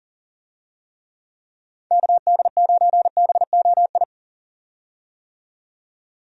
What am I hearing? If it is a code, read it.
Morse code, heard as a keyed tone, about 40 wpm, 700 Hz. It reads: KD9BOI
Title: USA Call Signs - 6 Characters - 40wpm